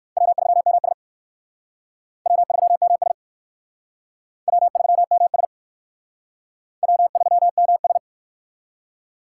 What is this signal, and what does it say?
Morse code, heard as a keyed tone, about 56 wpm, 700 Hz. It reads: W3MS W3MS W3MS W3MS
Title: Speed Racing - USA 1x2 Call Signs 45wpm